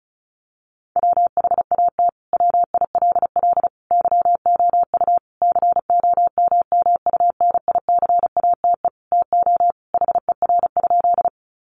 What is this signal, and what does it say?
Morse code, heard as a keyed tone, about 34 wpm, 700 Hz. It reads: WHAT WILL YOU COMMUNICATE TO HER?
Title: Sentences from Top 1300 Words 35wpm